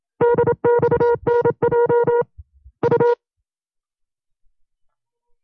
Morse code, heard as a keyed tone, about 27 wpm, 480 Hz. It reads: DXNJ U
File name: FreeFiles / CW-practice / 25WPM-side1